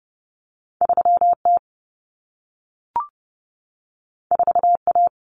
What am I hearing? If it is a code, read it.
Morse code, heard as a keyed tone, about 30 wpm, 700 Hz. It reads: 3T 4A
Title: Common Cut Numbers - 2-Digit 30wpm